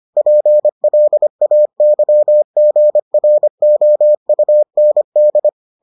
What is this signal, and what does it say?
Morse code, heard as a keyed tone, about 25 wpm, 600 Hz. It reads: PLAYGROUND